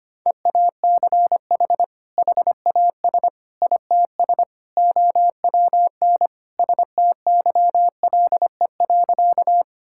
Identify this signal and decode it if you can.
Morse code, heard as a keyed tone, about 25 wpm, 700 Hz. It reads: EACH HAS ITS OWN STYLE.
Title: Sentences from Top 1000 Words - Review Part 5 - 25wpm